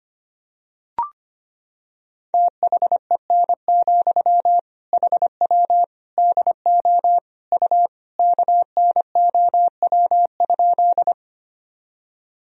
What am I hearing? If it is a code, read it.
Morse code, heard as a keyed tone, about 25 wpm, 700 Hz. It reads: THEN, HW DO U KNOW?
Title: Sentences from Top 100 Words and Abbreviations 25wpm